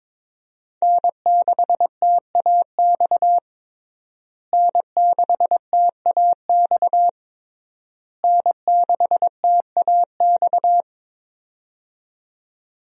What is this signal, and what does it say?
Morse code, heard as a keyed tone, about 22 wpm, 700 Hz. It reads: N6TAX N6TAX N6TAX
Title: USA Call Signs - 5 Characters - Repeated 3x - 22wpm